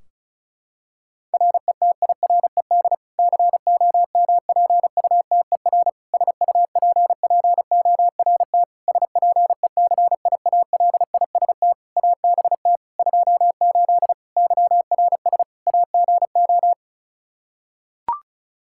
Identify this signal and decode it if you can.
Morse code, heard as a keyed tone, about 34 wpm, 700 Hz. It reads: RETIRED COMPUTER SUPPORT SPECIALIST ABT 28 YRS AGO
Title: Practice QSO - Exchange 3 - More Info - 35wpm